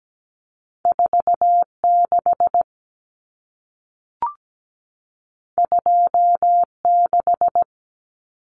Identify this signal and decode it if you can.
Morse code, heard as a keyed tone, about 17 wpm, 700 Hz. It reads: 46 26
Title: Common Cut Numbers - 2-Digit 17wpm